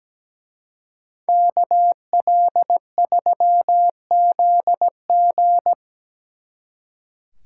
Morse code, heard as a keyed tone, about 20 wpm, 700 Hz. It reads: KL3ZG
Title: Speed Racing - USA 2x2 Call Signs 17wpm